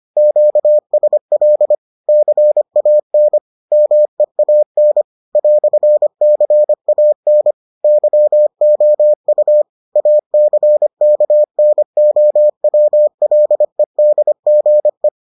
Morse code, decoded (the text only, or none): QSL CAN MEAN "CAN YOU ACKNOWLEDGE